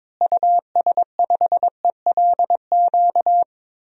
Morse code, heard as a keyed tone, about 22 wpm, 700 Hz. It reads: US5ELQ
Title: DXCC-100 Call Signs - 6 Characters - Repeated 2x - 22wpm